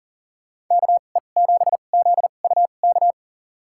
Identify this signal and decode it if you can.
Morse code, heard as a keyed tone, about 40 wpm, spaced as for 29 wpm, 700 Hz. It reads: KE7ZUK
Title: USA Call Signs - 6 Characters - 2x Character Spacing - 40wpm